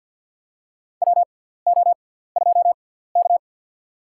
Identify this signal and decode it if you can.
Morse code, heard as a keyed tone, about 51 wpm, spaced as for 17 wpm, 700 Hz. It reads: WO2K
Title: USA Call Signs - 6x Character Spacing - 50wpm